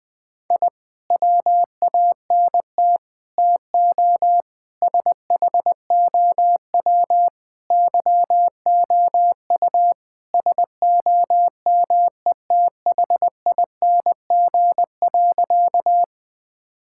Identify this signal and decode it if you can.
Morse code, heard as a keyed tone, about 21 wpm, 700 Hz. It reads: I WANT TO SHOW YOU SOMETHING.
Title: Sentences with 5-Word Phrases - Spoken Beginning - 20wpm